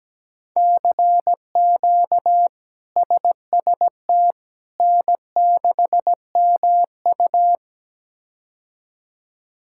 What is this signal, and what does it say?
Morse code, heard as a keyed tone, about 17 wpm, 700 Hz. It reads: CQ SST N6MU
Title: K1USN SST QSO Practice - 17wpm